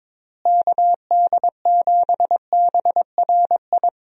Morse code, heard as a keyed tone, about 22 wpm, 700 Hz. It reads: KD7BRI
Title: USA Call Signs - 6 Characters - Repeated 3x - 22wpm